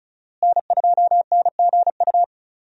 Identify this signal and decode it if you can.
Morse code, heard as a keyed tone, about 36 wpm, 700 Hz. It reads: N2NGU
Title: USA Call Signs - 5 Characters - Repeated 3x - 35wpm